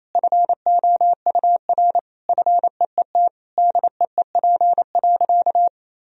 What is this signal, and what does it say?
Morse code, heard as a keyed tone, about 28 wpm, 700 Hz. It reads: FOUR FEET DEEP.
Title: Sentences from Top 1000 Words - Review Part 3 - 28wpm